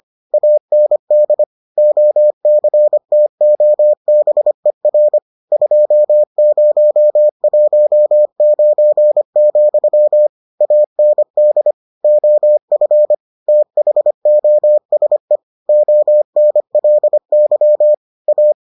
Morse code, heard as a keyed tone, about 25 wpm, 600 Hz. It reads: AND OCTOBER 2019, AND OF THOSE ONLY A